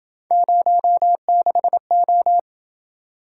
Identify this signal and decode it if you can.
Morse code, heard as a keyed tone, about 27 wpm, 700 Hz. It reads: 06O